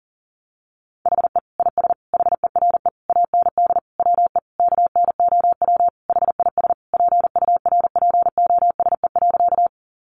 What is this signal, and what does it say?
Morse code, heard as a keyed tone, about 40 wpm, 700 Hz. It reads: HE IS HERE AND WE KNOW HIS PURPOSE.
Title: Sentences from Top 800 Words 40wpm